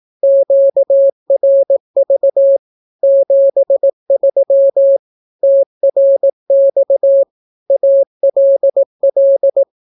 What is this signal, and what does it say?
Morse code, heard as a keyed tone, about 18 wpm, 550 Hz. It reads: QRV 73 TRX ALL
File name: naukapodstaw / skroty / skroty